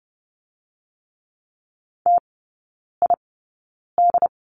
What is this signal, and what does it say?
Morse code, heard as a keyed tone, about 30 wpm, spaced as for 9 wpm, 700 Hz. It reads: TID
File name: Lesson 17e - Letter u - Three Characters 30wpm